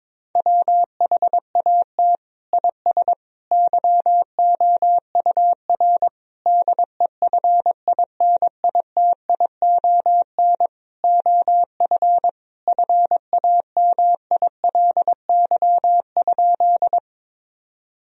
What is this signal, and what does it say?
Morse code, heard as a keyed tone, about 22 wpm, 700 Hz. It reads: WHAT IS YOUR DEFINITION OF FAMILY?
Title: Sentences from Top 1000 Words - Review Part 4 - 22wpm